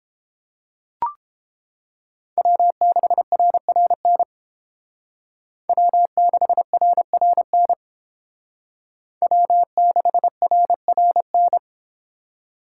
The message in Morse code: W6RRN W6RRN W6RRN

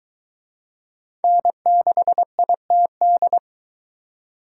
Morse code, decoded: N6ITD

